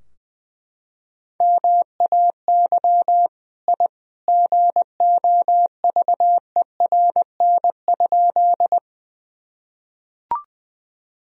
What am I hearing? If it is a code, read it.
Morse code, heard as a keyed tone, about 20 wpm, 700 Hz. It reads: MAY I GOVERN?
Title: Sentences from Top 1000 Words - Review Part 2 - 20wpm